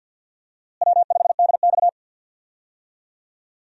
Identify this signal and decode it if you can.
Morse code, heard as a keyed tone, about 46 wpm, 700 Hz. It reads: W5DX
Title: USA Call Signs - 4 Characters - Repeated 2x - 50wpm